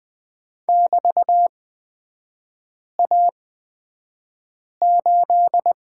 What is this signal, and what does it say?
Morse code, heard as a keyed tone, about 20 wpm, spaced as for 5 wpm, 700 Hz. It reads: <BT>A8
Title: 250 Lesson 40e - BK - Three Characters 20wpm